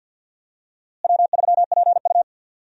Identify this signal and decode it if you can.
Morse code, heard as a keyed tone, about 51 wpm, 700 Hz. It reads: W3PU
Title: USA Call Signs - 4 Characters - Repeated 2x - 50wpm